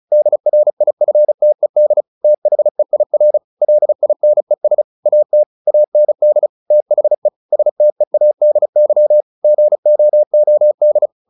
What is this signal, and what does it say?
Morse code, heard as a keyed tone, about 35 wpm, 600 Hz. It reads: DRIFTED THEIR LINES AT AND THE STEADY GOOD
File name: the old man and the sea / the old man and the sea 1 35wpm